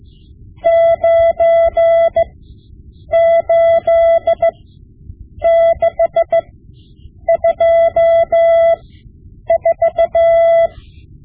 Morse code, heard as a keyed tone, about 15 wpm, spaced as for 8 wpm, 670 Hz. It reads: 98624
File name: Morse Lessons / Morse Lessons / Tape-5B